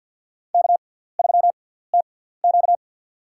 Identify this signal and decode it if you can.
Morse code, heard as a keyed tone, about 53 wpm, spaced as for 17 wpm, 700 Hz. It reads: K3TQ
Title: USA Call Signs - 6x Character Spacing - 50wpm